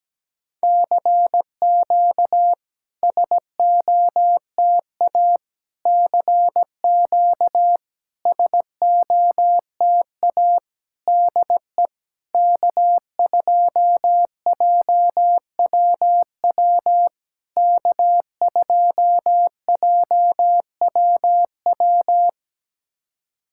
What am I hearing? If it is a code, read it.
Morse code, heard as a keyed tone, about 17 wpm, 700 Hz. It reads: CQ SOTA CQ SOTA DE K2JWW K2JWW
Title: SOTA QSO Practice 17wpm